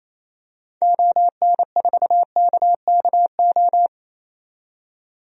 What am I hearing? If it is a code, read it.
Morse code, heard as a keyed tone, about 28 wpm, 700 Hz. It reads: ON4KKO
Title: DXCC-100 Call Signs - 6 Characters - 28wpm